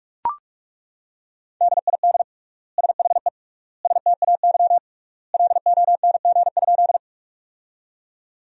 Morse code, heard as a keyed tone, about 42 wpm, 700 Hz. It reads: DID SHE STAY LONG?
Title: 3x Word-Spacing - Sentences from Top 500 Words - 45wpm